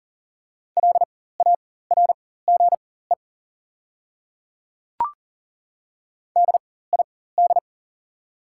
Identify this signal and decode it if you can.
Morse code, heard as a keyed tone, about 40 wpm, spaced as for 19 wpm, 690 Hz. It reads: LARGE DID
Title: 4x Character Spacing - Top 200 Words - 40wpm